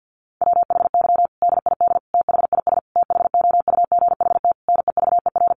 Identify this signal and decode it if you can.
Morse code, heard as a keyed tone, about 50 wpm, 700 Hz. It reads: WHY DID THIS THOUGHT NEVER
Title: Sentences from Top 900 Words 50wpm